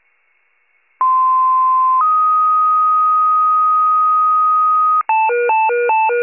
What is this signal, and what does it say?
Two-tone page, heard as a one-second tone, then a three-second tone, about 1040 then 1250 Hz. It is heard from 1.0 to 5.0 s.